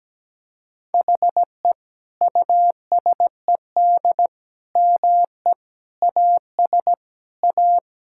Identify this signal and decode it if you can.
Morse code, heard as a keyed tone, about 17 wpm, 700 Hz. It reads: HE USED ME AS A
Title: Sentences from Top 1000 Words - Review Part 6 - 17wpm